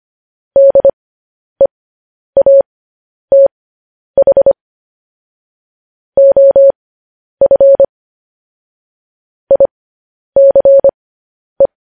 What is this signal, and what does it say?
Morse code, heard as a keyed tone, about 25 wpm, spaced as for 10 wpm, 550 Hz. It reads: DEATH OF ICE
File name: podcast / News headlines-1x-25x10